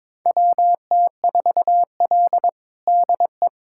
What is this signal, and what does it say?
Morse code, heard as a keyed tone, about 22 wpm, 700 Hz. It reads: WT4L DE
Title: USA Callsign DE Callsign - 4 Characters - 22wpm